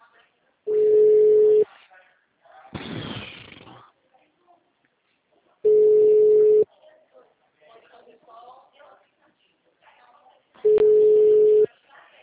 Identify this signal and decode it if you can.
Ringback tone (Europe, 425 Hz), with speech.
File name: out- -2004- - - .42171